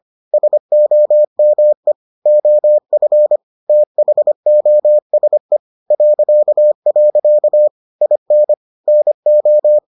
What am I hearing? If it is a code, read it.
Morse code, heard as a keyed tone, about 25 wpm, 600 Hz. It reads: SOME OF THOSE .. IN NO